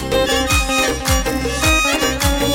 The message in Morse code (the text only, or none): none